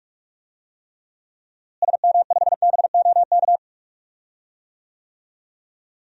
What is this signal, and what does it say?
Morse code, heard as a keyed tone, about 46 wpm, 700 Hz. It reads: SM5BOK
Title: DXCC-100 Call Signs - 6 Characters - Repeated 3x - 45wpm